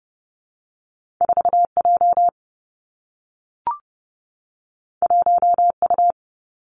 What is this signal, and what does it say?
Morse code, heard as a keyed tone, about 30 wpm, 700 Hz. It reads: 4 J 1 U
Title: Two Challenging Letter-Number Combinations 30wpm